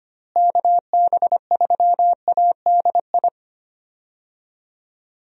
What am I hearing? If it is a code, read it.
Morse code, heard as a keyed tone, about 25 wpm, 700 Hz. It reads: KB3ADI
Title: USA Call Signs - 6 Characters - Repeated 2x - 25wpm